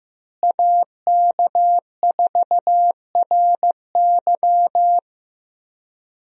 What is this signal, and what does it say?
Morse code, heard as a keyed tone, about 15 wpm, 700 Hz. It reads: AK4RY
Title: USA Call Signs - 5 Characters - Repeated 3x - 15wpm